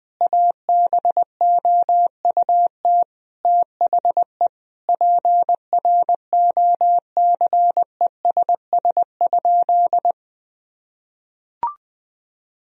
Morse code, heard as a keyed tone, about 20 wpm, 700 Hz. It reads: ABOUT THE PROCESS?
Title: Sentences from Top 1000 Words - Review Part 3 - 20wpm